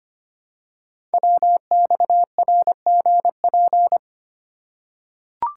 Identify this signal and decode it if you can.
Morse code, heard as a keyed tone, about 25 wpm, 700 Hz. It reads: WXRGP
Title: Code Groups - 5 Letters - 25wpm